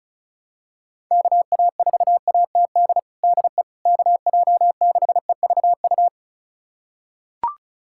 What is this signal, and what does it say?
Morse code, heard as a keyed tone, about 34 wpm, 700 Hz. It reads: KA4ATD DE KJ6EVU
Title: USA Callsign DE Callsign - 6 Characters - 35wpm